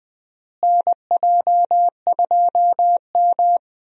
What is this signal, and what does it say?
Morse code, heard as a keyed tone, about 21 wpm, 700 Hz. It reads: NJ2M
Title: USA Call Signs - 4 Characters - Repeated 2x - 20wpm